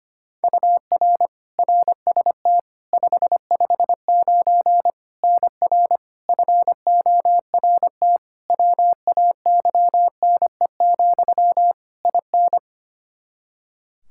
Morse code, heard as a keyed tone, about 25 wpm, 700 Hz. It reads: UR RST 559 NR FORT WAYNE, IN
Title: Practice QSO - Exchange 1 - Beginning - 25wpm